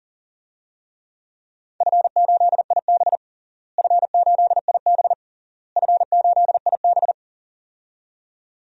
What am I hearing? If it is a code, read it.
Morse code, heard as a keyed tone, about 40 wpm, 700 Hz. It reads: F8IB F8IB F8IB
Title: DXCC-100 Call Signs - 4 Characters - Repeated 3x - 40wpm